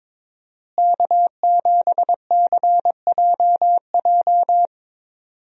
Morse code, heard as a keyed tone, about 22 wpm, 700 Hz. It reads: K7CJJ
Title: USA Call Signs - 5 Characters - Repeated 3x - 22wpm